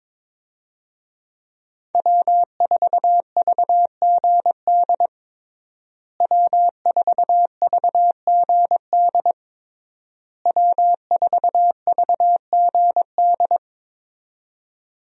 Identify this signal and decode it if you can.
Morse code, heard as a keyed tone, about 22 wpm, 700 Hz. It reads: W4VGD W4VGD W4VGD
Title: USA Call Signs - 5 Characters - Repeated 3x - 22wpm